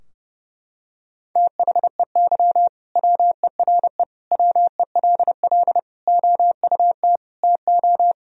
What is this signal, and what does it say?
Morse code, heard as a keyed tone, about 32 wpm, 700 Hz. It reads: THEY WERE WELL OUT TO
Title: Sentences from Top 1000 Words - Review Part 3 - 30wpm